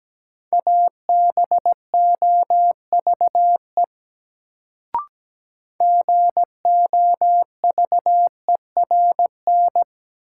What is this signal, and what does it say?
Morse code, heard as a keyed tone, about 17 wpm, 700 Hz. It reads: ABOVE GOVERN
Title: Sentences from Top 1000 Words - Review Part 2 - New Word Drill - 17wpm